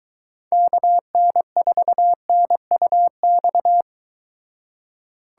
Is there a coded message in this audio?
Yes: Morse code from 0.5 to 3.8 s, heard as a keyed tone, about 23 wpm, 700 Hz. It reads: KN4NUX